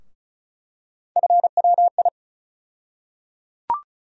Morse code, heard as a keyed tone, about 34 wpm, 700 Hz. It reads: FWI
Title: USA Call Sign Postfixes - 3 Letters 35wpm